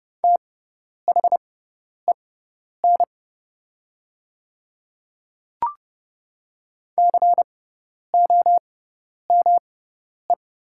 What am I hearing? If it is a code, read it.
Morse code, heard as a keyed tone, about 30 wpm, spaced as for 10 wpm, 690 Hz. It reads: THEN COME
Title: 6x Character Spacing - Top 100 Words - 30wpm